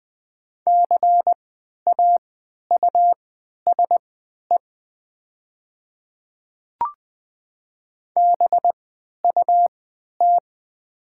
Morse code, heard as a keyed tone, about 20 wpm, spaced as for 11 wpm, 700 Hz. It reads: CAUSE BUT
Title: 3x Character Spacing - Top 200 Words - 20wpm